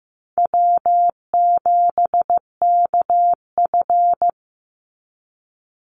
Morse code, heard as a keyed tone, about 15 wpm, 700 Hz. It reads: W7KF
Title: Easy Call Signs Repeated 3x - 15wpm